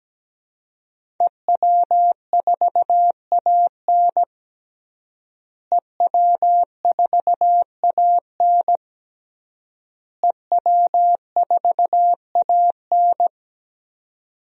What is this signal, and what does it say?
Morse code, heard as a keyed tone, about 17 wpm, 700 Hz. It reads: EW4AN EW4AN EW4AN
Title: DXCC-100 Call Signs - 5 Characters - Repeated 3x - 17wpm